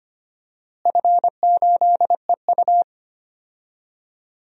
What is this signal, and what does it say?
Morse code, heard as a keyed tone, about 25 wpm, 700 Hz. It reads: F8EU
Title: DXCC-100 Call Signs - 4 Characters - Repeated 2x - 25wpm